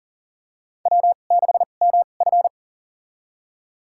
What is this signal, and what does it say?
Morse code, heard as a keyed tone, about 37 wpm, spaced as for 28 wpm, 690 Hz. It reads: W6MF
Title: USA Call Signs - 4 Characters - 2x Character Spacing - 40wpm